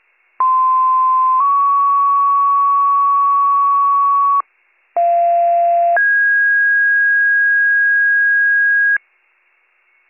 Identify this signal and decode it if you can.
Two-tone page, heard as a one-second tone, then a three-second tone, about 1040 then 1120 Hz, about 690 then 1650 Hz, with noise. It is heard from 0.4 to 9.0 s.